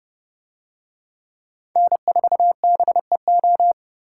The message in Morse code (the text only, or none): N4BEO